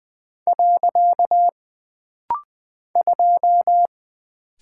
Morse code, heard as a keyed tone, about 20 wpm, 700 Hz. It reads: . 2
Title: Single Character (Letter:Number:Punctuation:Prosign) - Rapid-Fire - 20wpm